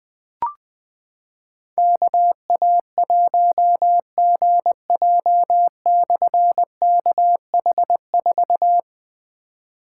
Morse code, heard as a keyed tone, about 20 wpm, 700 Hz. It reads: KA1GJ/KH4
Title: Difficult Call Signs 20wpm